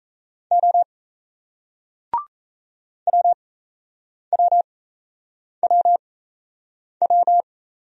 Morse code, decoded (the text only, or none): O WWWW